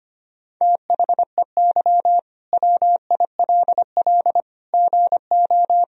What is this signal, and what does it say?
Morse code, heard as a keyed tone, about 25 wpm, 700 Hz. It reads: THEY WILL GO
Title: Sentences from Top 1000 Words - Review Part 4 - 25wpm